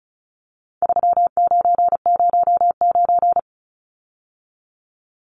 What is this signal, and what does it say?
Morse code, heard as a keyed tone, about 34 wpm, 700 Hz. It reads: 3909
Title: Numbers - 4-Digit 35wpm